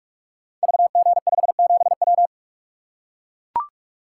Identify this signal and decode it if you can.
Morse code, heard as a keyed tone, about 44 wpm, 700 Hz. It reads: VG57W
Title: Code Groups - 5 Letter:Numbers - 45wpm